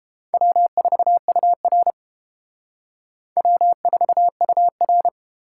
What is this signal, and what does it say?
Morse code, heard as a keyed tone, about 30 wpm, 700 Hz. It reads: W4UR W4UR